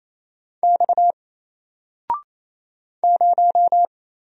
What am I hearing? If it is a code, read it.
Morse code, heard as a keyed tone, about 29 wpm, 700 Hz. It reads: X 0
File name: Single-Letter-Number-28wpm